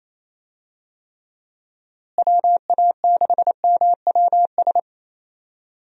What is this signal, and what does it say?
Morse code, heard as a keyed tone, about 28 wpm, 700 Hz. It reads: WA6MWS
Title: USA Call Signs - 6 Characters - Repeated 3x - 28wpm